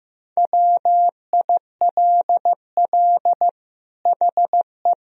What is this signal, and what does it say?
Morse code, heard as a keyed tone, about 15 wpm, 700 Hz. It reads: WILL HE
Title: Sentences from Top 1000 Words - Review Part 3 - 15wpm